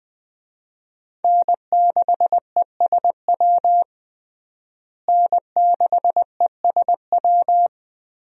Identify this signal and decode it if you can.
Morse code, heard as a keyed tone, about 20 wpm, 700 Hz. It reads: N6ESW N6ESW
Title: USA Call Signs - 5 Characters - Repeated 2x - 20wpm